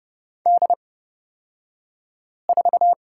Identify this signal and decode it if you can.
Morse code, heard as a keyed tone, about 30 wpm, spaced as for 5 wpm, 700 Hz. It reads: D4
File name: Lesson 36d - BT - Two Characters 30wpm